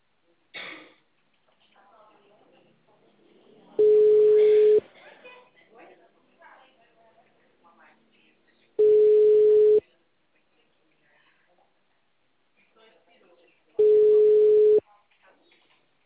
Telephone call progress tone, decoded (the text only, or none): ringback tone